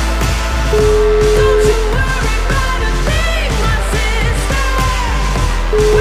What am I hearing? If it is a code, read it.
Ringback tone (Europe, 425 Hz), with noise.